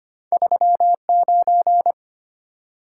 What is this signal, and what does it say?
Morse code, heard as a keyed tone, about 25 wpm, 700 Hz. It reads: 39
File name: Two Letter-Number Combinations - 25wpm